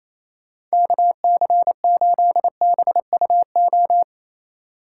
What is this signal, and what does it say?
Morse code, heard as a keyed tone, about 32 wpm, 700 Hz. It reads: KC8BUO
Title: Speed Racing - USA Call Signs 28wpm